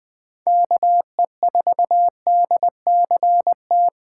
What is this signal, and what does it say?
Morse code, heard as a keyed tone, about 20 wpm, 700 Hz. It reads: KE4DCT
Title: USA Call Signs - 6 Characters - 2x Character Spacing - 20wpm